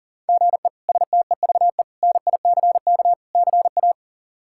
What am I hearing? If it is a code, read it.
Morse code, heard as a keyed tone, about 40 wpm, 700 Hz. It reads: GE STEVE NICK CA
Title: K1USN SST QSO Practice - 40wpm